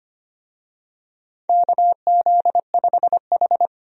Morse code, heard as a keyed tone, about 25 wpm, 700 Hz. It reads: KZ5H
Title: USA Call Signs - 4 Characters - 25wpm